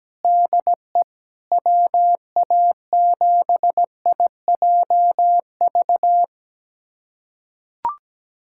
Morse code, heard as a keyed tone, about 17 wpm, 700 Hz. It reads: DE WA7IJV
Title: USA Callsign DE Callsign - 6 Characters - 17wpm